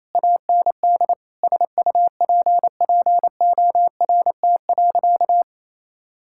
Morse code, heard as a keyed tone, about 27 wpm, 700 Hz. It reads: AND SUPPORT.
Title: Sentences from Top 1000 Words - Review Part 5 - 28wpm